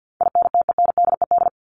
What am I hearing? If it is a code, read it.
Morse code, heard as a keyed tone, about 50 wpm, 700 Hz. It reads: INTENDED
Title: ED Suffix - Top 100 Words with Base Words 50wpm